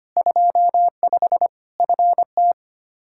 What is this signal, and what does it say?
Morse code, heard as a keyed tone, about 25 wpm, 700 Hz. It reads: 25 FT
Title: K1USN SST QSO Practice 25wpm